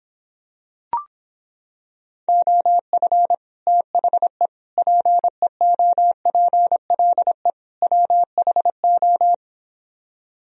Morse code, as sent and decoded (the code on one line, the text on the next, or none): --- ..-. / - .... / .--. --- .--. .-.. / .-- .... ---
OF THE PEOPLE WHO